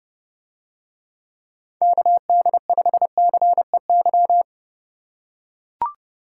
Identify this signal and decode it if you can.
Morse code, heard as a keyed tone, about 30 wpm, 700 Hz. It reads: KD5CEY